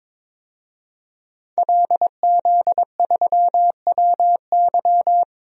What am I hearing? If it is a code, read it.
Morse code, heard as a keyed tone, about 22 wpm, 700 Hz. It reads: LZ3WY
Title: DXCC-100 Call Signs - 5 Characters - 22wpm